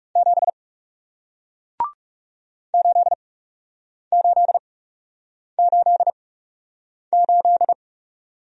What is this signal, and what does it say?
Morse code, heard as a keyed tone, about 40 wpm, spaced as for 8 wpm, 700 Hz. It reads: 7 8888